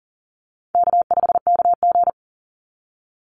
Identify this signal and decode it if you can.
Morse code, heard as a keyed tone, about 40 wpm, 700 Hz. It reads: K5KG
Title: Easy Call Signs Repeated 2x - 40wpm